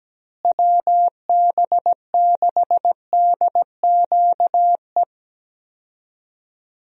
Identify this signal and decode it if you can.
Morse code, heard as a keyed tone, about 17 wpm, 700 Hz. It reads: WB6DQE
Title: USA Call Signs - 6 Characters - Repeated 2x - 17wpm